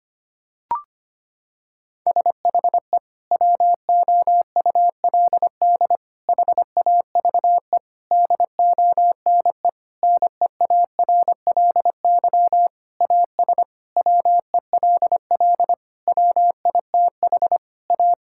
Morse code, heard as a keyed tone, about 25 wpm, 700 Hz. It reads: SHE WOULD HAVE DONE NEARLY AS WELL WITH A
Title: Sentences from Top 1600 Words 25wpm